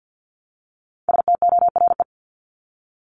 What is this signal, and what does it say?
Morse code, heard as a keyed tone, about 50 wpm, 700 Hz. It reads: STORE
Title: Top 501-750 Words 50wpm